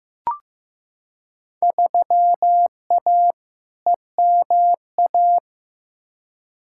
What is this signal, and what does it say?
Morse code, heard as a keyed tone, about 15 wpm, 700 Hz. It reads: 3A EMA
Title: Field Day Exchanges - 15wpm